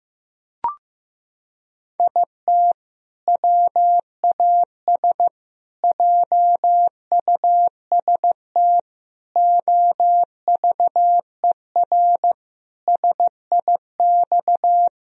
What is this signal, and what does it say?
Morse code, heard as a keyed tone, about 15 wpm, 700 Hz. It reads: IT WAS JUST OVER SIX
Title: Sentences from Top 1000 Words - Review Part 2 - 15wpm